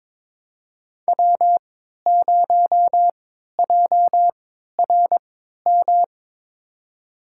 Morse code, heard as a keyed tone, about 22 wpm, spaced as for 12 wpm, 700 Hz. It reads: W0JRM
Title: USA Call Signs - 5 Characters - 3x Character Spacing - 22wpm